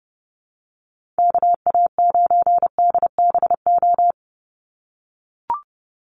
Morse code, heard as a keyed tone, about 30 wpm, 700 Hz. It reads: KA9DBO
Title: USA Call Signs 30wpm